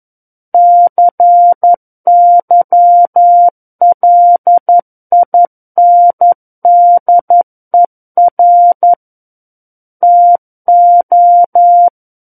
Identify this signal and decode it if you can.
Morse code, heard as a keyed tone, about 11 wpm, 700 Hz. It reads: CYLINDER TO
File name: 11 WPM